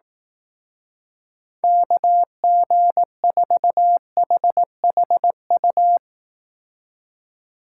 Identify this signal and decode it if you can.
Morse code, heard as a keyed tone, about 19 wpm, 700 Hz. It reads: KG4HHU
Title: Speed Racing - USA 2x3 Call Signs 15wpm